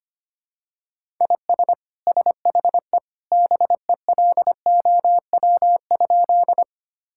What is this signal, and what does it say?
Morse code, heard as a keyed tone, about 25 wpm, 700 Hz. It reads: IS SHE BELOW?
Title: Sentences from Top 1000 Words - Review Part 5 - 25wpm